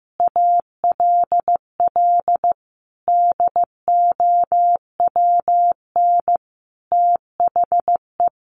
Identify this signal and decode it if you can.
Morse code, heard as a keyed tone, about 15 wpm, 700 Hz. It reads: ALL DOWN THE
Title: A Sound Of Thunder 15wpm